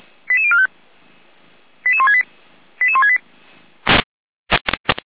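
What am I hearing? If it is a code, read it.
Five-tone selective call (ZVEI), heard as quick runs of five tones, tones of about 80 ms. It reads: 88045 88168 88168
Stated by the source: ZVEI five-tone call received by SDR, logged by feuerwehr-orlamuende (Orlamünde fire brigade).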